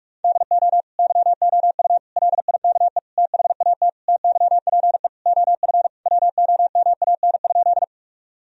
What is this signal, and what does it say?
Morse code, heard as a keyed tone, about 46 wpm, 700 Hz. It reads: DO YOU LIKE THAT TYPE OF WOMAN?
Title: Sentences from Top 1000 Words - Review Part 3 - 45wpm